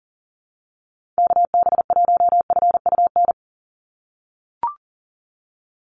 Morse code, heard as a keyed tone, about 38 wpm, 700 Hz. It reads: KB1FUN
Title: USA Call Signs 40wpm